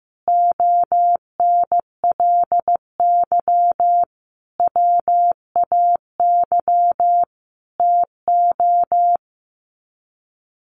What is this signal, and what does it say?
Morse code, heard as a keyed tone, about 15 wpm, 700 Hz. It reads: ONLY WAY TO
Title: Sets of 5 Words Encore 15wpm